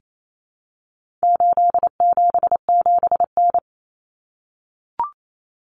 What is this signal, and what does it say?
Morse code, heard as a keyed tone, about 28 wpm, 700 Hz. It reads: 877N
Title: Common Cut Numbers - 4-Digit 28wpm